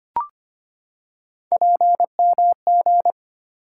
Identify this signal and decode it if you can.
Morse code, heard as a keyed tone, about 25 wpm, 700 Hz. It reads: PMG
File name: Three Letter-Number Combinations - 25wpm